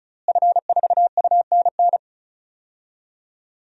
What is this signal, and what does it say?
Morse code, heard as a keyed tone, about 36 wpm, 700 Hz. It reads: F4UNN
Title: DXCC-100 Call Signs - 5 Characters - Repeated 2x - 35wpm